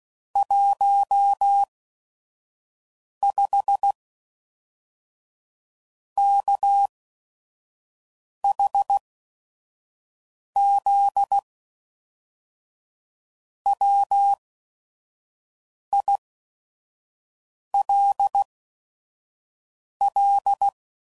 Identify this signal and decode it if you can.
Morse code, heard as a keyed tone, about 16 wpm, spaced as for 5 wpm, 800 Hz. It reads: 15 KHZ WILL
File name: CW Code Groups @5WPM